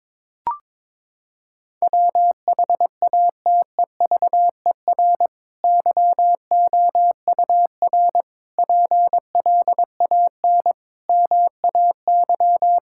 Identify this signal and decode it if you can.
Morse code, heard as a keyed tone, about 22 wpm, 700 Hz. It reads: WHATEVER YOUR PLAN MAY
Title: Sentences from Top 1600 Words 22wpm